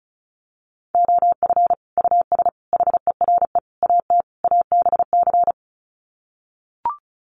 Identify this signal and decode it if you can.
Morse code, heard as a keyed tone, about 34 wpm, 700 Hz. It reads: OF US HERE AT ABC
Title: Sets of 5 Words 35wpm